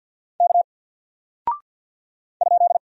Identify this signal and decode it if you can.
Morse code, heard as a keyed tone, about 50 wpm, 700 Hz. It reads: K ?
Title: Single Character (Letter:Number:Punctuation:Prosign) Alt 50wpm